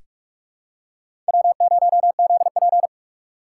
Morse code, heard as a keyed tone, about 46 wpm, 700 Hz. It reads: W0ZP
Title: K1USN SST QSO Practice - 45wpm